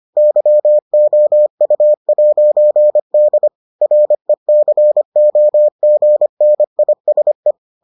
Morse code, heard as a keyed tone, about 25 wpm, 600 Hz. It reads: YOU'D RECOGNISE